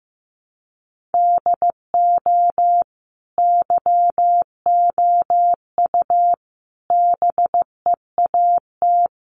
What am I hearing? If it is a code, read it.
Morse code, heard as a keyed tone, about 15 wpm, 700 Hz. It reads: DO YOU BEAT
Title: Sentences from Top 1300 Words 15wpm